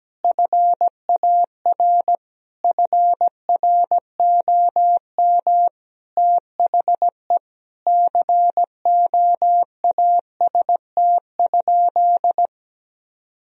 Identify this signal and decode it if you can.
Morse code, heard as a keyed tone, about 17 wpm, 700 Hz. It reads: FAR FROM THE COAST?
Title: Sentences from Top 1000 Words - Review Part 4 - 17wpm